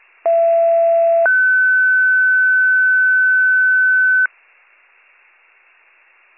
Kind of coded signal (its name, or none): two-tone page